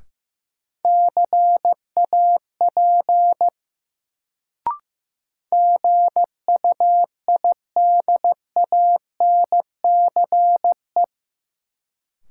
Morse code, heard as a keyed tone, about 15 wpm, 700 Hz. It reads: CAP GUIDANCE
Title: Sentences from Top 1600 Words - New Word Drill - 15wpm